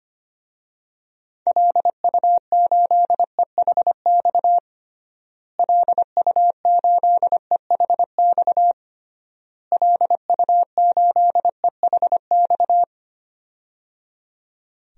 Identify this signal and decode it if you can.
Morse code, heard as a keyed tone, about 25 wpm, 700 Hz. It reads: LU8EHX LU8EHX LU8EHX
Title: DXCC-100 Call Signs - 6 Characters - Repeated 3x - 25wpm